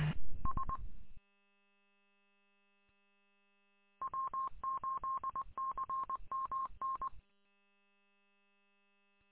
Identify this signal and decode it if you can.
Morse code, heard as a keyed tone, about 20 wpm, 1070 Hz. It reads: S W8CMN
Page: W8CMN 146.780 Fenton